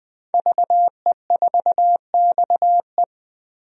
Morse code, heard as a keyed tone, about 20 wpm, 700 Hz. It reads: VE4XE
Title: DXCC-100 Call Signs - 5 Characters - Repeated 2x - 20wpm